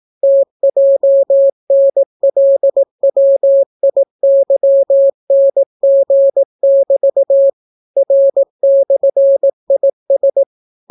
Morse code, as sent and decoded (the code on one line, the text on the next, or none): - .--- -. .-.. .-- .. -.-- -. --. -...- / .-. -..-. .. ...
TJNLWIYNG<BT> R/IS